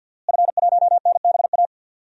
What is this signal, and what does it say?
Morse code, heard as a keyed tone, about 52 wpm, 700 Hz. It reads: F1NBA
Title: DXCC-100 Call Signs - 5 Characters - Repeated 2x - 50wpm